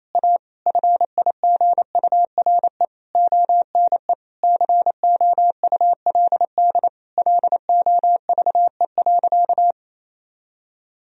Morse code, heard as a keyed tone, about 28 wpm, 700 Hz. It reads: A FIGURE ONE COULD LOVE.
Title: Sentences from Top 1000 Words - Review Part 2 - 28wpm